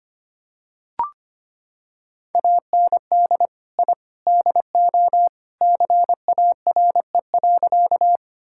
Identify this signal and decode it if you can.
Morse code, heard as a keyed tone, about 25 wpm, 700 Hz. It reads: AND I DO CARE.
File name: practice / 25wpm / Sentences with 3-Word Phrases - 25wpm